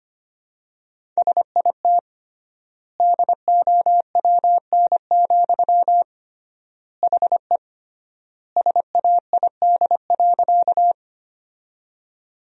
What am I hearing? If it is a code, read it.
Morse code, heard as a keyed tone, about 25 wpm, 700 Hz. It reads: SIT DOWN, HE SAID.
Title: 3x Word-Spacing - Sentences from Top 500 Words - 25wpm